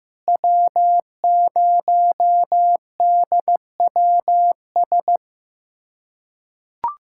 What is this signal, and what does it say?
Morse code, heard as a keyed tone, about 15 wpm, 700 Hz. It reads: W0DWS
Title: USA Callsign DE Callsign - 5 Characters - 15wpm